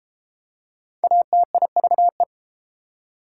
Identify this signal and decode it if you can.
Morse code, heard as a keyed tone, about 32 wpm, 700 Hz. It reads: ATIVE